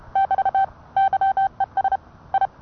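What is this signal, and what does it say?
Morse code, heard as a keyed tone, about 30 wpm, 780 Hz. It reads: <BT> YES I
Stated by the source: Morse code sent by N1EA on a Vibroplex key.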